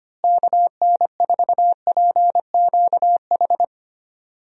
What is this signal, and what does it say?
Morse code, heard as a keyed tone, about 25 wpm, 700 Hz. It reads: KN4PQH
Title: USA Call Signs - 6 Characters - Repeated 2x - 25wpm